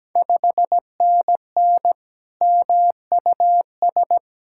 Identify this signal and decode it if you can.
Morse code, heard as a keyed tone, about 17 wpm, 700 Hz. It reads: 5NN MUS
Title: Ontario QSO Party Exchange 17wpm